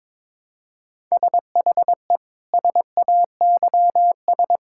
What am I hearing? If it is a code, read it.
Morse code, heard as a keyed tone, about 22 wpm, 700 Hz. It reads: SHE SAYS
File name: practice / 22wpm / Sentences with 2-Word Phrases - 22wpm